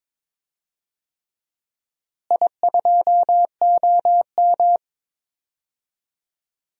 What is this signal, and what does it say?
Morse code, heard as a keyed tone, about 22 wpm, 700 Hz. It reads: I2OM
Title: DXCC-100 Call Signs - 4 Characters - Repeated 3x - 22wpm